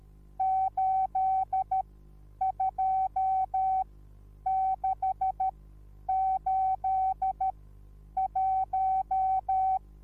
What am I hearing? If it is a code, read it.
Morse code, heard as a keyed tone, about 13 wpm, spaced as for 9 wpm, 760 Hz. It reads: 82681
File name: Onlinelehrgang / morsen / DJ4UF Morsen21